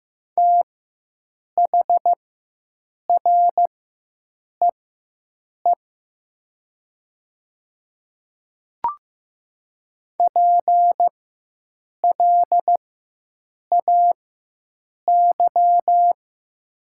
Morse code, heard as a keyed tone, about 15 wpm, spaced as for 7 wpm, 700 Hz. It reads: THREE PLAY
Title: 4x Character Spacing - Top 200 Words - 15wpm